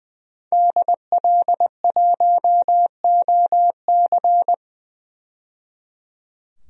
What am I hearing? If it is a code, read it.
Morse code, heard as a keyed tone, about 20 wpm, 700 Hz. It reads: DL1OC
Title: DXCC-100 Call Signs - 5 Characters - 20wpm